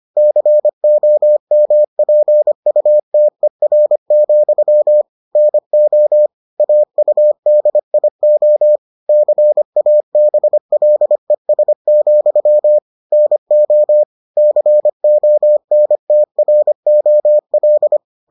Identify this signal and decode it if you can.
Morse code, heard as a keyed tone, about 25 wpm, 600 Hz. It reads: COMPUTER, NO AUDIO CABLES, NO CONTROL